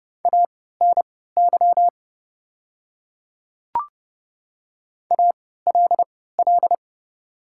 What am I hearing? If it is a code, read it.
Morse code, heard as a keyed tone, about 29 wpm, spaced as for 17 wpm, 690 Hz. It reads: ANY ALL
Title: 3x Character Spacing - Top 200 Words - 30wpm